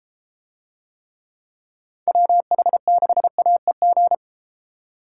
Morse code, heard as a keyed tone, about 32 wpm, 700 Hz. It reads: WH6AEG